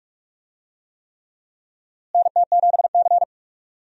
Morse code, heard as a keyed tone, about 41 wpm, 700 Hz. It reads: NT7C
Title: USA Call Signs - 4 Characters - Repeated 3x - 45wpm